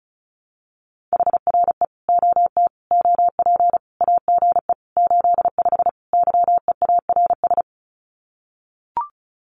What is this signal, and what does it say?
Morse code, heard as a keyed tone, about 34 wpm, 700 Hz. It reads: HRE OT OP AGE 85 YEARS
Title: Common QSO Phrases 35wpm